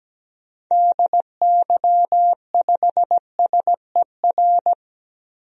Morse code, heard as a keyed tone, about 17 wpm, 700 Hz. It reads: DY5SER
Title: DXCC-100 Call Signs - 6 Characters - 17wpm